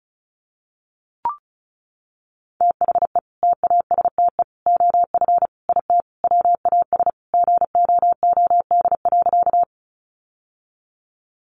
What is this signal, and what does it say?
Morse code, heard as a keyed tone, about 34 wpm, 700 Hz. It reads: THE TASTE OF IT WAS GOOD.
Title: Sentences from Top 1100 Words 35wpm